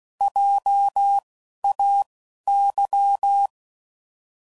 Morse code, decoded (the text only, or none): JAY